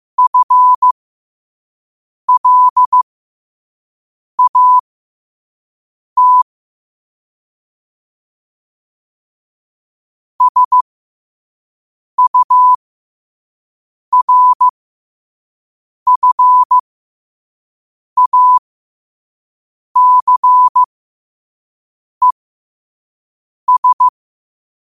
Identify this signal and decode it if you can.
Morse code, heard as a keyed tone, about 15 wpm, spaced as for 5 wpm, 1000 Hz. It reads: FLAT SURFACES
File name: qotdfiles / 5wpm